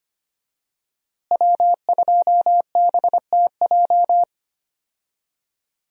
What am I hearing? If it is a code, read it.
Morse code, heard as a keyed tone, about 25 wpm, 700 Hz. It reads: W2BTJ
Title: USA Call Signs - 5 Characters - Repeated 2x - 25wpm